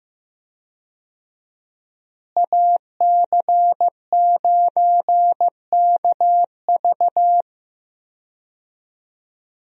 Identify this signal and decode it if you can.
Morse code, heard as a keyed tone, about 15 wpm, 700 Hz. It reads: AC9KV
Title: USA Call Signs - 5 Characters - Repeated 2x - 15wpm